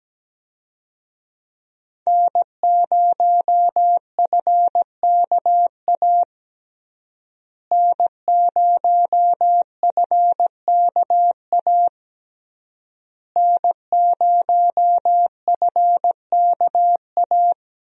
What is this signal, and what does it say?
Morse code, heard as a keyed tone, about 17 wpm, 700 Hz. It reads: N0FKA N0FKA N0FKA
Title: USA Call Signs - 5 Characters - Repeated 3x - 17wpm